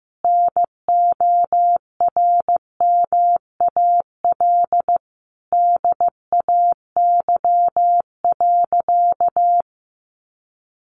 Morse code, heard as a keyed tone, about 15 wpm, 700 Hz. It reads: NORMAL DAY.
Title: Sentences from Top 1200 Words 15wpm